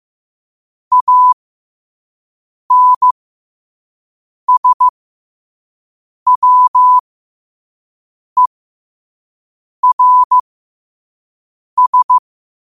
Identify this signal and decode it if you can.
Morse code, heard as a keyed tone, about 15 wpm, spaced as for 5 wpm, 1000 Hz. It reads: ANSWERS
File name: qotdfiles / 5wpm